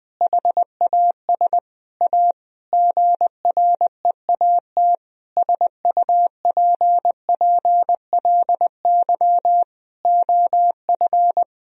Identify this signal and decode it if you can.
Morse code, heard as a keyed tone, about 20 wpm, 700 Hz. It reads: HAS A GREAT SUPPLY OF
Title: Sentences from Top 1000 Words - Review Part 4 - 20wpm